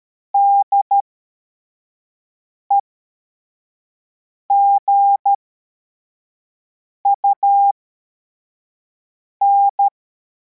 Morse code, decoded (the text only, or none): DEGUN